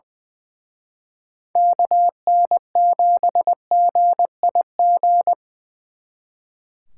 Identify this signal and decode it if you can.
Morse code, heard as a keyed tone, about 25 wpm, 700 Hz. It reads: KN7GIG
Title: Speed Racing - USA 2x3 Call Signs 20wpm